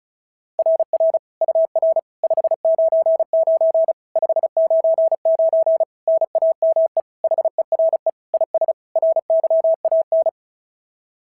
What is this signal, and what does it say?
Morse code, heard as a keyed tone, about 36 wpm, 690 Hz. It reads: RR UR 599 599 NAME HERE IS RYAN
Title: SOTA QSO Practice 35wpm